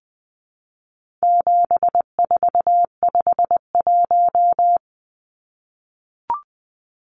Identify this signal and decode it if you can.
Morse code, heard as a keyed tone, about 20 wpm, 700 Hz. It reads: 7451
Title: Numbers - 4-Digit 20wpm